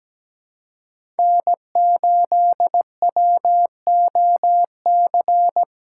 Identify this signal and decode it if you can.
Morse code, heard as a keyed tone, about 17 wpm, 700 Hz. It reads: N8WOC
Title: USA Callsign DE Callsign - 5 Characters - 17wpm